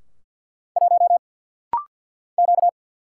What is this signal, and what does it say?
Morse code, heard as a keyed tone, about 49 wpm, 700 Hz. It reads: 1 Q
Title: Single Letter-Number Alt - Rapid-Fire 50wpm